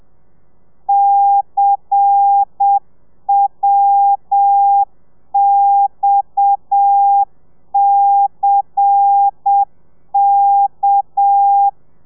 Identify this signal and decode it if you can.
Morse code, heard as a keyed tone, about 7 wpm, 800 Hz. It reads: CWXCK
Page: Lezione 1: audio a 30 CPM